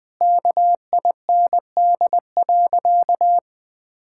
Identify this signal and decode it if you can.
Morse code, heard as a keyed tone, about 21 wpm, 700 Hz. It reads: KIND.
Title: Sentences with 8-Word Phrases - 20wpm